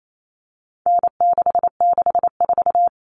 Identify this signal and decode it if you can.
Morse code, heard as a keyed tone, about 28 wpm, 700 Hz. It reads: N664
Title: All Cut Numbers - 4-Digit 28wpm